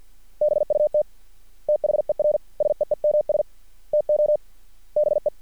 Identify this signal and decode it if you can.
Morse code, heard as a keyed tone, about 50 wpm, 600 Hz. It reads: BUT THER SEEMS TO BE
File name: archived / CRSnet